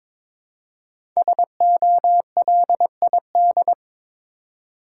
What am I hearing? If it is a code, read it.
Morse code, heard as a keyed tone, about 21 wpm, 700 Hz. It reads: SOLID
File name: 5 Letter Words - First 3 Letters Given - 22wpm